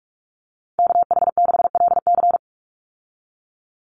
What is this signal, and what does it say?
Morse code, heard as a keyed tone, about 44 wpm, 700 Hz. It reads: KH6LC
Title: Call Signs Repeated 3x - 45wpm